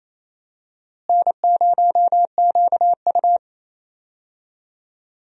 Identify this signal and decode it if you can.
Morse code, heard as a keyed tone, about 28 wpm, 700 Hz. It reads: N0QU
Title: USA Call Signs - 4 Characters - Repeated 2x - 28wpm